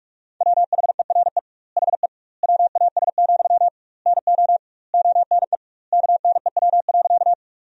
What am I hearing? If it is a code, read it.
Morse code, heard as a keyed tone, about 44 wpm, 700 Hz. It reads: WHERE HE WAS, NO ONE KNEW.
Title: 2x Word-Spacing - Sentences from Top 400 Words - 45wpm